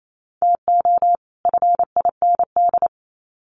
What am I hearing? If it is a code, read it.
Morse code, heard as a keyed tone, about 28 wpm, 700 Hz. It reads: TO FIND
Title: Sets of 2 Words 28wpm